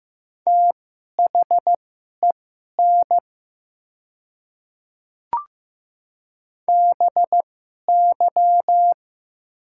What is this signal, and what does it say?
Morse code, heard as a keyed tone, about 15 wpm, spaced as for 11 wpm, 700 Hz. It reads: THEN BY
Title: 2x Character Spacing - Top 100 Words - 15wpm